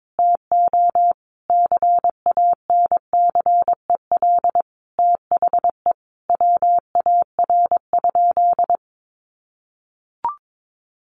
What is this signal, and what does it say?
Morse code, heard as a keyed tone, about 22 wpm, 700 Hz. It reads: TO CANCEL THE WAR?
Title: Sentences from Top 1200 Words 22wpm